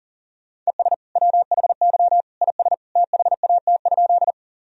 Morse code, heard as a keyed tone, about 40 wpm, 700 Hz. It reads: ES WHY IS THAT?
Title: Sentences from Top 200 Words and Abbreviations 40wpm